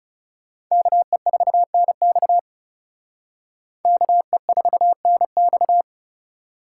Morse code, heard as a keyed tone, about 37 wpm, 700 Hz. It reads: KE4NX KE4NX
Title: Speed Racing - USA Call Signs 30wpm